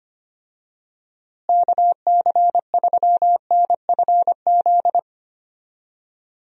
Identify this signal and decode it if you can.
Morse code, heard as a keyed tone, about 25 wpm, 700 Hz. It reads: KC3NFZ
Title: USA Call Signs - 6 Characters - Repeated 3x - 25wpm